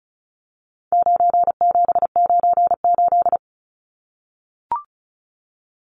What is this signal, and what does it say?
Morse code, heard as a keyed tone, about 34 wpm, 700 Hz. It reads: 9798
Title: Numbers - 4-Digit 35wpm